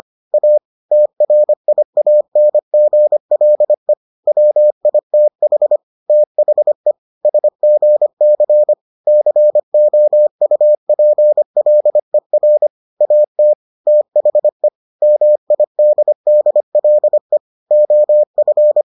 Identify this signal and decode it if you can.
Morse code, heard as a keyed tone, about 25 wpm, 600 Hz. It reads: A TRIANGLE WITH THE SGC COUPLER AT THE MIDDLE OF